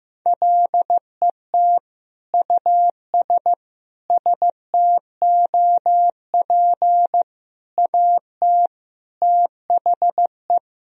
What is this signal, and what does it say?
Morse code, heard as a keyed tone, about 15 wpm, 700 Hz. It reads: LET US STOP AT THE
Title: Sentences from Top 400 Words and Abbreviations 15wpm